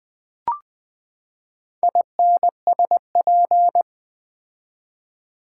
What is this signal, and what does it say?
Morse code, heard as a keyed tone, about 20 wpm, 700 Hz. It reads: INSP
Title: Build A Word - 4 Letters At A Time - 10-Letter Words - 20wpm